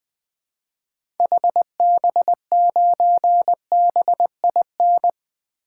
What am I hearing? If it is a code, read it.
Morse code, heard as a keyed tone, about 21 wpm, 700 Hz. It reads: HB9BIN
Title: 2023 SOTA Activator Call Signs - 20wpm